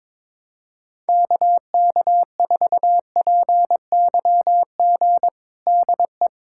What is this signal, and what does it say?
Morse code, heard as a keyed tone, about 22 wpm, 700 Hz. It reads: KK4PYG DE
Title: USA Callsign DE Callsign 22wpm